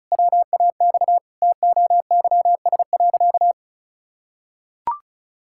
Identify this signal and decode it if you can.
Morse code, heard as a keyed tone, about 34 wpm, 700 Hz. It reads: WAX TOYS.
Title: Pangram Challenge v2 - 35wpm